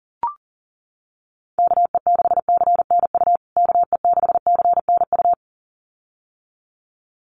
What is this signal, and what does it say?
Morse code, heard as a keyed tone, about 37 wpm, 700 Hz. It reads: KE6CNU KE6CNU
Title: USA Call Signs Repeated 2x 40wpm